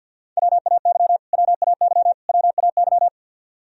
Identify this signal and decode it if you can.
Morse code, heard as a keyed tone, about 51 wpm, 700 Hz. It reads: WAY WAY WAY
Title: Top 100 Words Repeated 50wpm